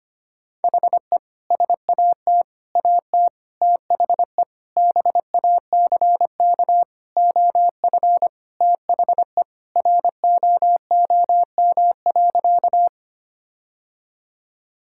Morse code, heard as a keyed tone, about 25 wpm, 700 Hz. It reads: HE SAT AT THE BACK OF THE ROOM.
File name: practice / 25wpm / Sentences with 3-Word Phrases - 25wpm